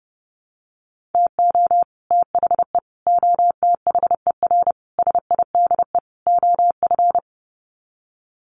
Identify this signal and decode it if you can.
Morse code, heard as a keyed tone, about 30 wpm, 700 Hz. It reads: TO THE OTHER SIDE OF
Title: Sets of 5 Words Encore 30wpm